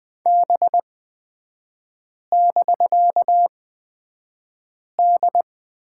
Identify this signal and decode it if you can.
Morse code, heard as a keyed tone, about 20 wpm, spaced as for 5 wpm, 700 Hz. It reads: B<BK>D
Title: 250 Lesson 40e - BK - Three Characters 20wpm